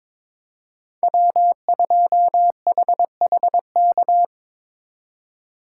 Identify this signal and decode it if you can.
Morse code, heard as a keyed tone, about 26 wpm, 700 Hz. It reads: W2HHK
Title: Speed Racing - USA 1x3 Call Signs 22wpm